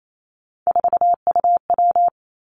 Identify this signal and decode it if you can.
Morse code, heard as a keyed tone, about 28 wpm, 700 Hz. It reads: 4UW
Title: Three Challenging Letter-Number Combinations 28wpm